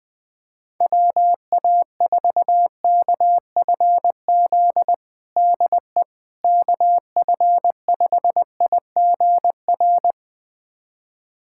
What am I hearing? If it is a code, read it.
Morse code, heard as a keyed tone, about 20 wpm, 700 Hz. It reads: WA4KFZ DE KF5IGR
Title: USA Callsign DE Callsign - 6 Characters - 20wpm